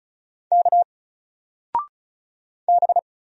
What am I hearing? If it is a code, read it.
Morse code, heard as a keyed tone, about 34 wpm, 700 Hz. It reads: K B
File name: Single-Letter-Number-35wpm